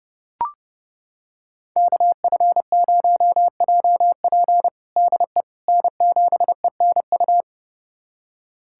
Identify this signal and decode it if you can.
Morse code, heard as a keyed tone, about 30 wpm, 700 Hz. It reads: KF0JP DE N7ENU
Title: USA Callsign DE Callsign - 5 Characters - 30wpm